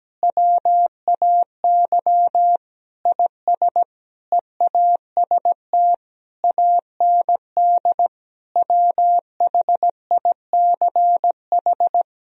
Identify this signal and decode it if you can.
Morse code, heard as a keyed tone, about 17 wpm, 700 Hz. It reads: WAY IS EAST AND WHICH
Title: Sentences from Top 1000 Words - Review Part 3 - 17wpm